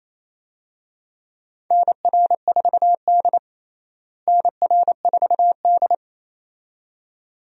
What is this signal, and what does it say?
Morse code, heard as a keyed tone, about 28 wpm, 700 Hz. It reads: NR4D NR4D
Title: USA Call Signs - 4 Characters - Repeated 2x - 28wpm